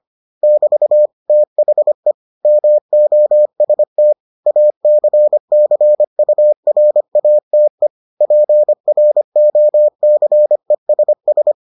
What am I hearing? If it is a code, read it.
Morse code, heard as a keyed tone, about 25 wpm, 600 Hz. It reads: <BT> THE MOST ACCURATE PROCESS